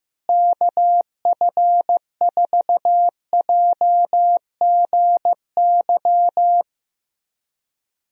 Morse code, heard as a keyed tone, about 15 wpm, 700 Hz. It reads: KF4JGY
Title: USA Call Signs - 6 Characters - Repeated 2x - 15wpm